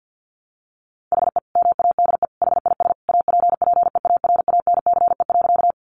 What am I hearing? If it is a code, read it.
Morse code, heard as a keyed tone, about 50 wpm, 700 Hz. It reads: HE MADE HIS APPEARANCE.
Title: Sentences from Top 1100 Words 50wpm